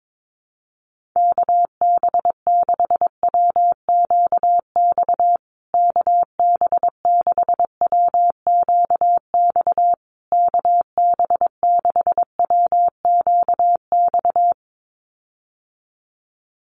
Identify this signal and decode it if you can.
Morse code, heard as a keyed tone, about 22 wpm, 700 Hz. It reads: KB6WQX KB6WQX KB6WQX
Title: USA Call Signs Repeated 3x 22wpm